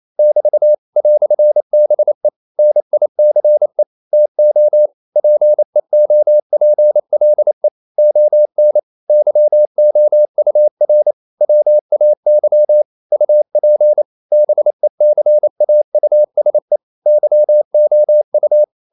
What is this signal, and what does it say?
Morse code, heard as a keyed tone, about 28 wpm, 600 Hz. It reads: <BT> "BE NICE TO PEOPLE ON YOUR WAY UP BECAUSE YOU
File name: CW / Practice / EnglishQuotations0228